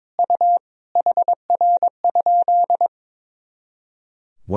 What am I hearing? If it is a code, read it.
Morse code, heard as a keyed tone, about 22 wpm, 700 Hz. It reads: U HR?
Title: Sentences from Top 500 Words and Abbreviations 22wpm